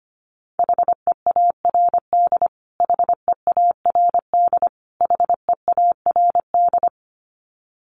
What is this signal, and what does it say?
Morse code, heard as a keyed tone, about 25 wpm, 700 Hz. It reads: HEARD HEARD HEARD
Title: Top 500 Words Repeated 25wpm